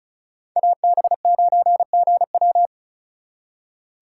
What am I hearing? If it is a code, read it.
Morse code, heard as a keyed tone, about 43 wpm, 700 Hz. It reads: AB9GW
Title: Speed Racing - USA 2x2 Call Signs 35wpm